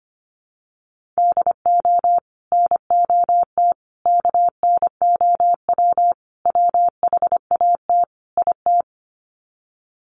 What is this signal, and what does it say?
Morse code, heard as a keyed tone, about 25 wpm, 700 Hz. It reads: DO NOT KNOW WHAT IT
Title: Sets of 5 Words 25wpm